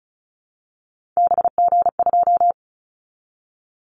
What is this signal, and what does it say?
Morse code, heard as a keyed tone, about 34 wpm, 700 Hz. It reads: BG2
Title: Callsign Prefixes for Top 100 DXCC - 35wpm